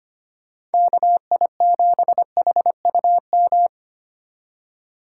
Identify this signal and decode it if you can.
Morse code, heard as a keyed tone, about 31 wpm, 700 Hz. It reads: KI7HUM
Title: Speed Racing - USA 2x3 Call Signs 25wpm